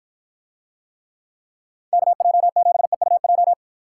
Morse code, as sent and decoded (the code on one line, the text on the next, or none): -.- .--- --... ..- .---
KJ7EUJ